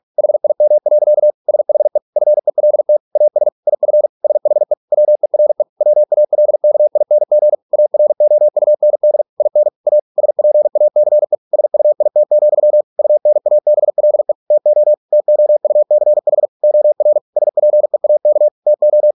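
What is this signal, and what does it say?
Morse code, heard as a keyed tone, about 46 wpm, 600 Hz. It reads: HIM. SHE FELT AS IF SHE WERE WALKING AROUND IN A SPACE SUIT, UNABLE TO TOUCH OR SPEAK TO
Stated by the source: Morse practice at 45 wpm, the novel Never by Ken Follett in English, from CW med Gnister.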